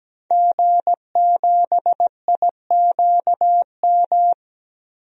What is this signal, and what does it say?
Morse code, heard as a keyed tone, about 17 wpm, 700 Hz. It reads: G7IQM
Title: DXCC-100 Call Signs - 5 Characters - 17wpm